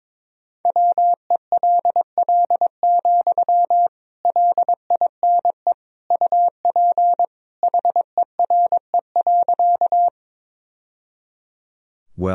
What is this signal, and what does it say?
Morse code, heard as a keyed tone, about 22 wpm, 700 Hz. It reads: WELL, LINE UP HERE.
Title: Sentences from Top 1000 Words - Review Part 1 - 22wpm